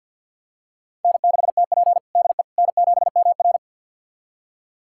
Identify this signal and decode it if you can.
Morse code, heard as a keyed tone, about 50 wpm, 690 Hz. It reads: N6TP DE N7MR
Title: USA Callsign DE Callsign - 4 Characters - 50wpm